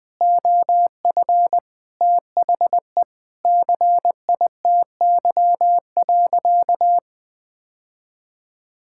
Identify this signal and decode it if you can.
Morse code, heard as a keyed tone, about 20 wpm, 700 Hz. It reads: OF THE CITY.
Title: Sentences from Top 1000 Words - Review Part 5 - 20wpm